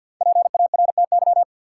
Morse code, heard as a keyed tone, about 51 wpm, 700 Hz. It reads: PARTY
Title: Sentences from Top 1400 Words - New Word Drill Alt - 50wpm